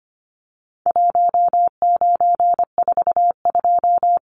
Morse code, heard as a keyed tone, about 25 wpm, 700 Hz. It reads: 1942
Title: Numbers - Contest Style - 4-Digit 25wpm